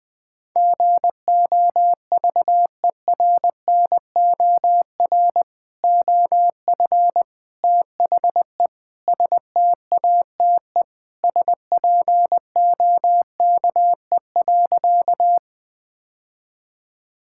Morse code, heard as a keyed tone, about 20 wpm, 700 Hz. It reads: GOVERNOR OF THE STATE SPOKE.
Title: Sentences with 3-Word Phrases - 20wpm